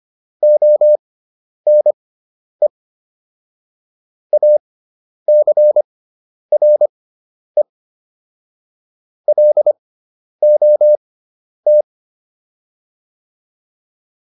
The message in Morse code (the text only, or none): ONE ACRE LOT